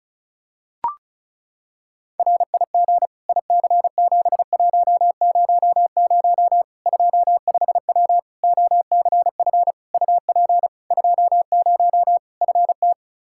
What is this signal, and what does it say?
Morse code, heard as a keyed tone, about 36 wpm, 700 Hz. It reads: RIG IC7100 25W OCF UP 20 FT
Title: K1USN SST QSO Practice 35wpm